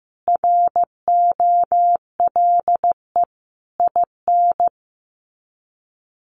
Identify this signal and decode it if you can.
Morse code, heard as a keyed tone, about 15 wpm, 700 Hz. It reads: ROLE IN
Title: Sets of 5 Words Encore2 15wpm